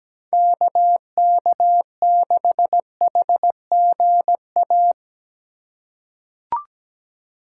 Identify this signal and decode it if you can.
Morse code, heard as a keyed tone, about 17 wpm, 700 Hz. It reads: KK6HGA
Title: USA Call Signs - 6 Characters - 3x Character Spacing - 17wpm